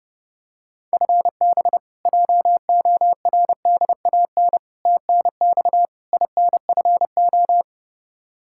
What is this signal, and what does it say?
Morse code, heard as a keyed tone, about 30 wpm, 700 Hz. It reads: FB JORDAN TNX INFO
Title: Practice QSO - Exchange 3 - More Info - 30wpm